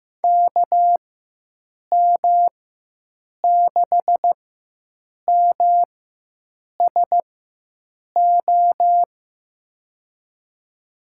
Morse code, heard as a keyed tone, about 15 wpm, spaced as for 7 wpm, 700 Hz. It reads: KM6MSO
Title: USA Call Signs - 6 Characters - 4x Character Spacing - 15wpm